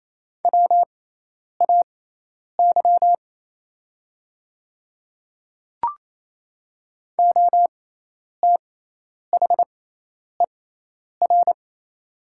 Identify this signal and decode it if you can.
Morse code, heard as a keyed tone, about 28 wpm, spaced as for 10 wpm, 700 Hz. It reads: WAY OTHER
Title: 6x Character Spacing - Top 100 Words - 28wpm